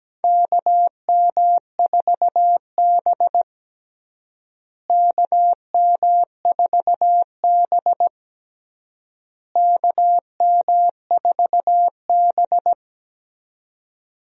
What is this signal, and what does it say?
Morse code, heard as a keyed tone, about 17 wpm, 700 Hz. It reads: KM4B KM4B KM4B
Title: USA Call Signs - 4 Characters - Repeated 3x - 17wpm